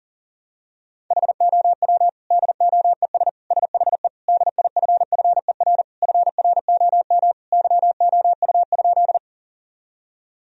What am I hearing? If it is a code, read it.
Morse code, heard as a keyed tone, about 40 wpm, 700 Hz. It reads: HOW DOES SHE DIFFER FROM YOU?
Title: Sentences from Top 1000 Words - Review Part 1 - 40wpm